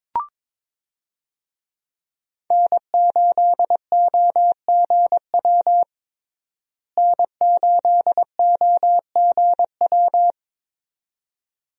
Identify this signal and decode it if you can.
Morse code, heard as a keyed tone, about 22 wpm, 700 Hz. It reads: N8OGW N8OGW
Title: USA Call Signs - 5 Characters - Repeated 2x - 22wpm